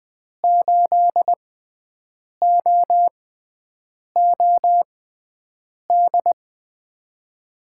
Morse code, heard as a keyed tone, about 20 wpm, spaced as for 7 wpm, 700 Hz. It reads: 8OOD
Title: USA Call Signs - 5 Characters - 6x Character Spacing - 20wpm